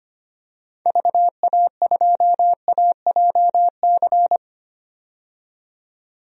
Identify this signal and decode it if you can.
Morse code, heard as a keyed tone, about 25 wpm, 700 Hz. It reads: VA2AJC
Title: DXCC-100 Call Signs - 6 Characters - 25wpm